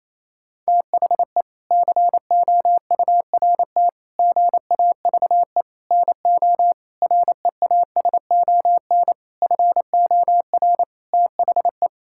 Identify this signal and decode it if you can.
Morse code, heard as a keyed tone, about 27 wpm, 700 Hz. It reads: THE COURT GAVE NO REASON FOR THE
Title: Sentences from Top 1600 Words 28wpm